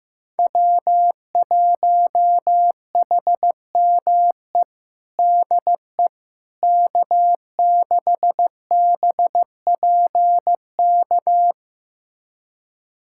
Morse code, heard as a keyed tone, about 15 wpm, 700 Hz. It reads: W1HME DE K6BPK
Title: USA Callsign DE Callsign - 5 Characters - 15wpm